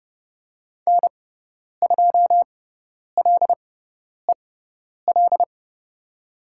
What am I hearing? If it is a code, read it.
Morse code, heard as a keyed tone, about 30 wpm, spaced as for 10 wpm, 700 Hz. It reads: N2LEL
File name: Lesson 16g - Letter c - Call Signs 30wpm